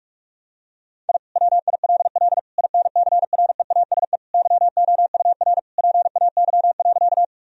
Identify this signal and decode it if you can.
Morse code, heard as a keyed tone, about 46 wpm, 700 Hz. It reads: I WILL INCREASE YOUR PAY.
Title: Sentences from Top 1000 Words - Review Part 4 - 45wpm